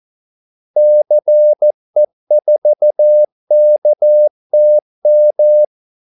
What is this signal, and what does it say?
Morse code, heard as a keyed tone, about 14 wpm, 600 Hz. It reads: CE4KTM
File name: Callsigns-14wpm